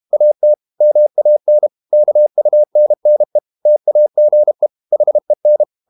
Morse code, heard as a keyed tone, about 32 wpm, 600 Hz. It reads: AT MAN KUNNE TAGE HEN
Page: Ferie 32 wpm